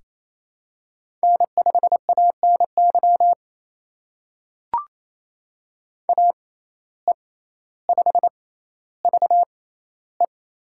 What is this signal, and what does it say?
Morse code, heard as a keyed tone, about 28 wpm, spaced as for 14 wpm, 700 Hz. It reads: N5ANY AE5VE
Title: USA Call Signs - 5 Characters - 6x Character Spacing - 28wpm